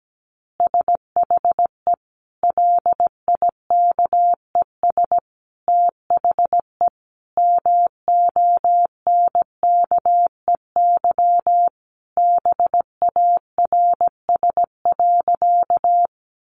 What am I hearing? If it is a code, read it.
Morse code, heard as a keyed tone, about 17 wpm, 700 Hz. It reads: SHE LIKES THE MONKEY BARS.
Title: Easy Phrases 17wpm